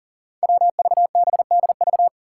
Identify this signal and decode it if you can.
Morse code, heard as a keyed tone, about 40 wpm, 700 Hz. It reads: WVBDV
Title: Five Challenging Letter-Number Combinations 40wpm